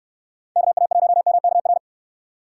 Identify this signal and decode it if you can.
Morse code, heard as a keyed tone, about 62 wpm, 700 Hz. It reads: KA1MKR